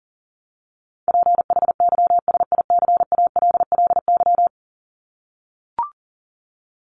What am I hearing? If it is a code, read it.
Morse code, heard as a keyed tone, about 40 wpm, 700 Hz. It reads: PHYSICALLY
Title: Top 1501-1750 Words 40wpm